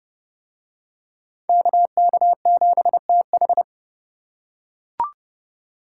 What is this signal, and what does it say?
Morse code, heard as a keyed tone, about 29 wpm, 700 Hz. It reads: KK7TH